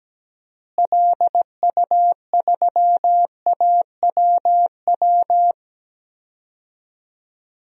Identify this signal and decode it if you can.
Morse code, heard as a keyed tone, about 17 wpm, 700 Hz. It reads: LU3AWW
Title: DXCC-100 Call Signs - 6 Characters - Repeated 2x - 17wpm